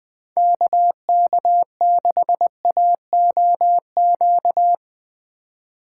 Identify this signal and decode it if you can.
Morse code, heard as a keyed tone, about 19 wpm, 700 Hz. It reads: KK6AOQ
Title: Speed Racing - USA 2x3 Call Signs 17wpm